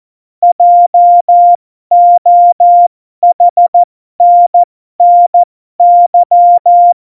Morse code, heard as a keyed tone, about 14 wpm, spaced as for 12 wpm, 700 Hz. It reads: JOHNNY
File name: Morsekurs / Witze